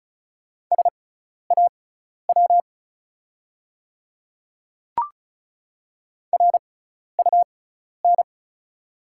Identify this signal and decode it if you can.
Morse code, heard as a keyed tone, about 36 wpm, spaced as for 12 wpm, 690 Hz. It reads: SAW RUN
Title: 6x Character Spacing - Top 500 Words - 35wpm